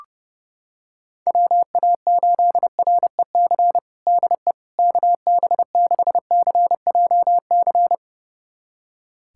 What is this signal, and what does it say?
Morse code, heard as a keyed tone, about 30 wpm, 700 Hz. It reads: WA8REC DE KB6CJC
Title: USA Callsign DE Callsign - 6 Characters - 30wpm